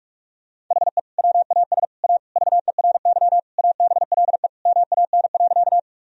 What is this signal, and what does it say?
Morse code, heard as a keyed tone, about 45 wpm, 700 Hz. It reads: HE WAS A VERY ABLE MAN.
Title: Sentences from Top 1000 Words - Review Part 2 - 45wpm